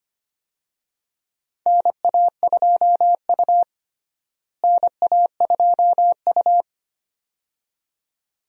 Morse code, heard as a keyed tone, about 25 wpm, 700 Hz. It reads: NA2U NA2U
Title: USA Call Signs - 4 Characters - Repeated 2x - 25wpm